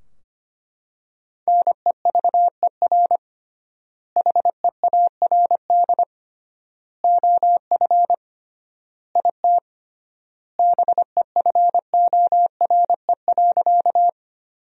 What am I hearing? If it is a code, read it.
Morse code, heard as a keyed tone, about 25 wpm, 700 Hz. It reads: NEVER HEARD OF IT BEFORE.
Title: 3x Word-Spacing - Sentences from Top 400 Words - 25wpm